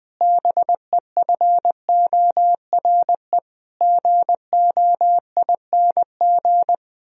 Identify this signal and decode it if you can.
Morse code, heard as a keyed tone, about 20 wpm, 700 Hz. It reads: BEFORE GOING
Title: Sentences from Top 1000 Words - Review Part 3 - 20wpm